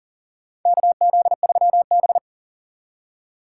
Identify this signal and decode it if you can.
Morse code, heard as a keyed tone, about 40 wpm, 700 Hz. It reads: KZ3B
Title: USA Call Signs - 4 Characters - 2x Character Spacing - 40wpm